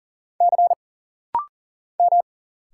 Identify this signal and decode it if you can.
Morse code, heard as a keyed tone, about 40 wpm, 700 Hz. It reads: C M
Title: Single Letters - Rapid-Fire 40wpm